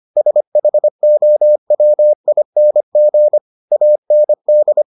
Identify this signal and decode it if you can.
Morse code, heard as a keyed tone, about 25 wpm, 600 Hz. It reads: SHOWING AND